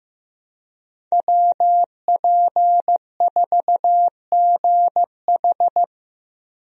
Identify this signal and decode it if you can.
Morse code, heard as a keyed tone, about 15 wpm, 700 Hz. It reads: WP4GH
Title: USA Call Signs - 5 Characters - 15wpm